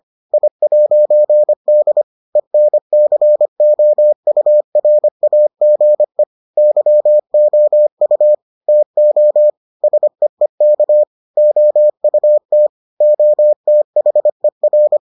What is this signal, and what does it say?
Morse code, heard as a keyed tone, about 25 wpm, 600 Hz. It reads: I'D ENCOURAGE YOU TO SEEK OUT OTHER